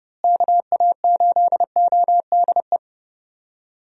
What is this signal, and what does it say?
Morse code, heard as a keyed tone, about 32 wpm, 700 Hz. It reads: KA8ODE
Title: USA Call Signs - 6 Characters - Repeated 2x - 30wpm